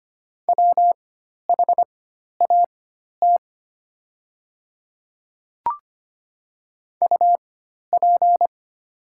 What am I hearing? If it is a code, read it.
Morse code, heard as a keyed tone, about 25 wpm, spaced as for 12 wpm, 700 Hz. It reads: WHAT UP
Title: 4x Character Spacing - Top 100 Words - 25wpm